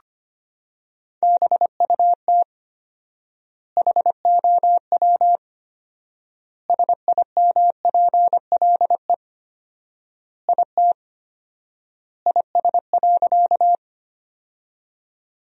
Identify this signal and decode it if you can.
Morse code, heard as a keyed tone, about 25 wpm, 700 Hz. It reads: BUT HOW SIMPLE IT IS.
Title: 4x Word-Spacing - Sentences from Top 400 Words - 25wpm